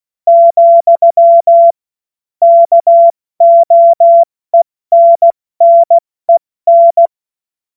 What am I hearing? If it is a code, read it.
Morse code, heard as a keyed tone, about 16 wpm, 680 Hz. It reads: , KOENNEN
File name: Morsekurs / Witze / Witz 25. Juli 2023 70 F80